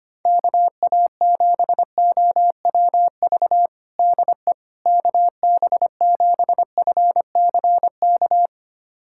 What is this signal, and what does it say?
Morse code, heard as a keyed tone, about 25 wpm, 700 Hz. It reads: KA7OWV DE KB7FCK
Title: USA Callsign DE Callsign - 6 Characters - 25wpm